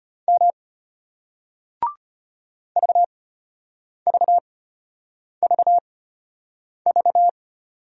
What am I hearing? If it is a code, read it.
Morse code, heard as a keyed tone, about 34 wpm, spaced as for 8 wpm, 700 Hz. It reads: M VVVV